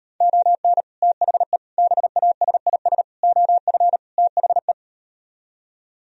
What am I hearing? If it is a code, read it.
Morse code, heard as a keyed tone, about 38 wpm, 700 Hz. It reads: ON THE BASIS OF THE